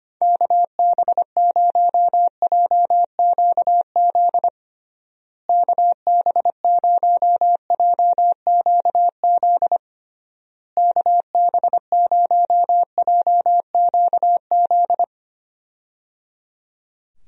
Morse code, heard as a keyed tone, about 25 wpm, 700 Hz. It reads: KB0JQZ KB0JQZ KB0JQZ
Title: USA Call Signs - 6 Characters - Repeated 3x - 25wpm